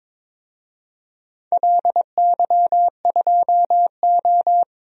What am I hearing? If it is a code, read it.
Morse code, heard as a keyed tone, about 22 wpm, 700 Hz. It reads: LY2O
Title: DXCC-100 Call Signs - 4 Characters - 22wpm